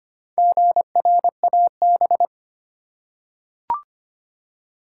Morse code, heard as a keyed tone, about 25 wpm, 700 Hz. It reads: GRAB
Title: Four Letter Words - 25wpm